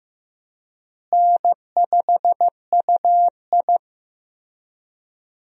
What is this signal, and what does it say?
Morse code, heard as a keyed tone, about 15 wpm, 700 Hz. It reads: N5UI
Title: USA Call Signs - 4 Characters - Repeated 3x - 15wpm